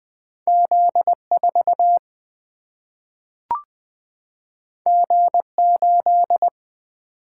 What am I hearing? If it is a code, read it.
Morse code, heard as a keyed tone, about 20 wpm, 700 Hz. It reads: Z4 G8
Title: Single-Letter Single-Number 20wpm